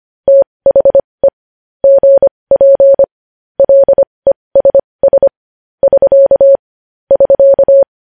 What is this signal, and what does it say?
Morse code, heard as a keyed tone, about 25 wpm, spaced as for 20 wpm, 550 Hz. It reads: THE GP LESS <SK> <SK>
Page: News Headlines in Morse Code at 20 WPM